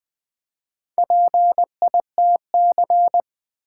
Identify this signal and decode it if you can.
Morse code, heard as a keyed tone, about 20 wpm, 700 Hz. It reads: PITC
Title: Build A Word - 1 Letter At A Time - 5-Letter Words - 20wpm